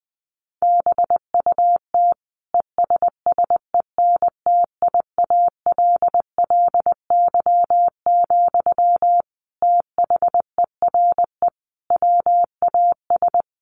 Morse code, heard as a keyed tone, about 20 wpm, 700 Hz. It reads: BUT ESSENTIALLY, THERE WAS
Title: Sentences from Top 1300 Words 20wpm